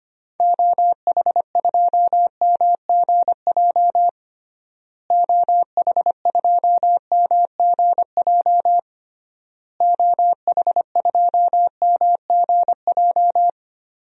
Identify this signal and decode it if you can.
Morse code, heard as a keyed tone, about 25 wpm, 700 Hz. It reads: OH2MGJ OH2MGJ OH2MGJ
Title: DXCC-100 Call Signs - 6 Characters - Repeated 3x - 25wpm